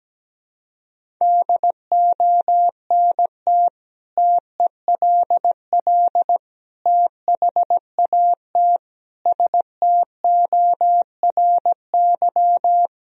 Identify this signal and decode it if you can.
Morse code, heard as a keyed tone, about 17 wpm, 700 Hz. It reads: DONT TELL THAT STORY
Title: Sentences from Top 1000 Words - Review Part 1 - 17wpm